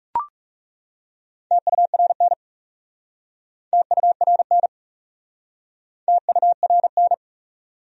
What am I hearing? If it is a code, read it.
Morse code, heard as a keyed tone, about 37 wpm, 700 Hz. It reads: TURN TURN TURN
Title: Speed-Racing Top 300 Words 30wpm